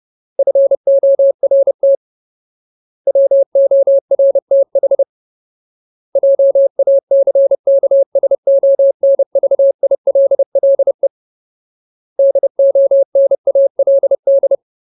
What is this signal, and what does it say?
Morse code, heard as a keyed tone, about 30 wpm, 550 Hz. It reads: FORT WORTH JACKSONVILLE DONALD
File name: Ultimate Word List / 30WPM Ultimate